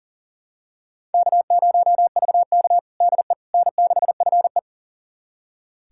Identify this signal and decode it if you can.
Morse code, heard as a keyed tone, about 40 wpm, 700 Hz. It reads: K0VK DE N6FE
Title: USA Callsign DE Callsign - 4 Characters - 40wpm